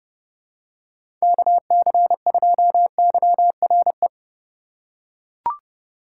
Morse code, heard as a keyed tone, about 29 wpm, 700 Hz. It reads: KC2YRE
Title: USA Call Signs - 6 Characters - 2x Character Spacing - 30wpm